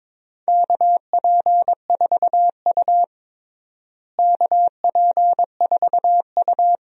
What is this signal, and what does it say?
Morse code, heard as a keyed tone, about 22 wpm, 700 Hz. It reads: KP4U KP4U
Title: DXCC-100 Call Signs - 4 Characters - Repeated 2x - 22wpm